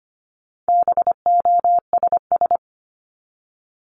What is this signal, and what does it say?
Morse code, heard as a keyed tone, about 25 wpm, 700 Hz. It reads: BOSS
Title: Top 501-750 Words 25wpm